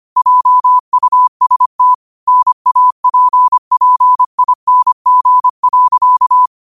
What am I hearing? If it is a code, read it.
Morse code, heard as a keyed tone, about 25 wpm, 1000 Hz. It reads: JUST NAPPING.
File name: qotdfiles / 25wpm